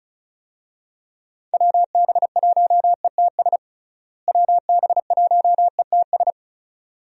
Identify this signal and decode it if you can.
Morse code, heard as a keyed tone, about 34 wpm, 700 Hz. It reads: WB1ETS WB1ETS
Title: USA Call Signs - 6 Characters - Repeated 2x - 35wpm